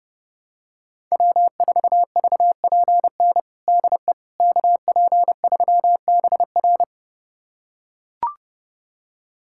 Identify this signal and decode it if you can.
Morse code, heard as a keyed tone, about 30 wpm, 700 Hz. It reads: W4VPN DE KP3BR
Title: USA Callsign DE Callsign - 5 Characters - 30wpm